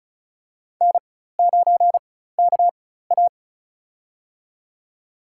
Morse code, heard as a keyed tone, about 36 wpm, spaced as for 16 wpm, 700 Hz. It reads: N9KA
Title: USA Call Signs - 4 Characters - 4x Character Spacing - 35wpm